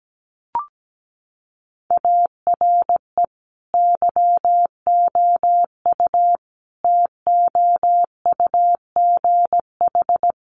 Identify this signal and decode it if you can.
Morse code, heard as a keyed tone, about 17 wpm, 700 Hz. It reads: ARE YOU TOUGH
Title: Sentences from Top 1200 Words 17wpm